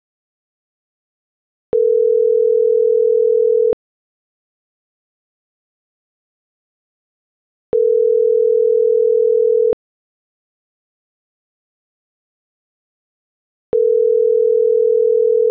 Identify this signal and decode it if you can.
Ringback tone (North America), 440 + 480 Hz.